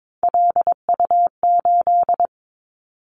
Morse code, heard as a keyed tone, about 22 wpm, 700 Hz. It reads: LU8
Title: Callsign Prefixes for Top 100 DXCC - 22wpm